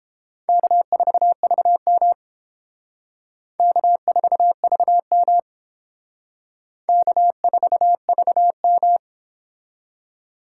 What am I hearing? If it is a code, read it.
Morse code, heard as a keyed tone, about 28 wpm, 700 Hz. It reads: K4VM K4VM K4VM